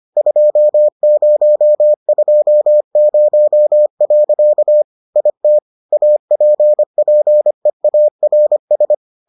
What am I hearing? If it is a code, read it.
Morse code, heard as a keyed tone, about 25 wpm, 600 Hz. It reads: 2020. IT APPEARS